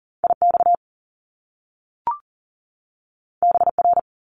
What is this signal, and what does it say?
Morse code, heard as a keyed tone, about 40 wpm, 700 Hz. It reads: IX BR
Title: Top Two Letter Combinations in Words 40wpm